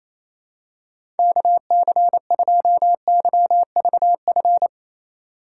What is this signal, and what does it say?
Morse code, heard as a keyed tone, about 28 wpm, 700 Hz. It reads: KC2YVF
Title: USA Call Signs - 6 Characters - 4x Character Spacing - 28wpm